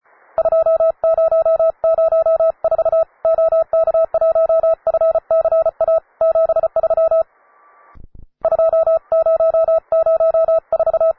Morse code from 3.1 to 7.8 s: OK1FCA 73